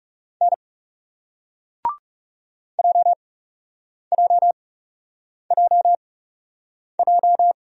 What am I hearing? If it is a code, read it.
Morse code, heard as a keyed tone, about 39 wpm, spaced as for 8 wpm, 700 Hz. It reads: N JJJJ